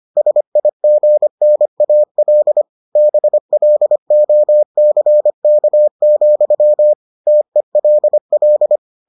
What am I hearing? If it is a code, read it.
Morse code, heard as a keyed tone, about 25 wpm, 600 Hz. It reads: SIGNAL BLOCK, TELL